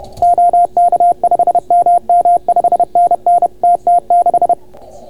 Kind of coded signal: Morse code